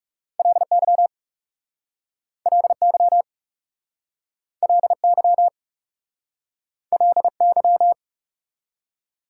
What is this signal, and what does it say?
Morse code, heard as a keyed tone, about 40 wpm, 700 Hz. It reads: LY LY LY LY